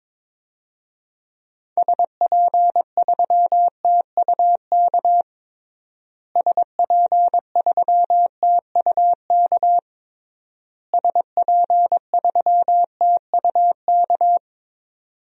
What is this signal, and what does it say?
Morse code, heard as a keyed tone, about 22 wpm, 700 Hz. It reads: SP3TUK SP3TUK SP3TUK
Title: DXCC-100 Call Signs - 6 Characters - Repeated 3x - 22wpm